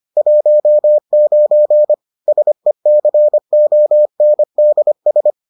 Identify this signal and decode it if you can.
Morse code, heard as a keyed tone, about 25 wpm, 600 Hz. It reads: 19 SECONDS